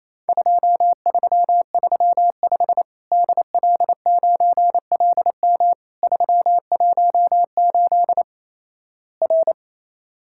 Morse code, heard as a keyed tone, about 28 wpm, 700 Hz. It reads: 2335 DL9LM 318 R
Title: Worked All Europe DX Contest - QTC Exchange - 28wpm